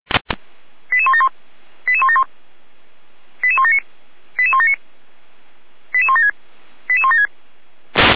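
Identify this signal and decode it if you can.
Five-tone selective call (ZVEI), heard as quick runs of five tones, tones of about 70 ms. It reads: 88171 88171 88179 88179 88176 88176
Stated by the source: Received by SDR; ZVEI five-tone code 89734.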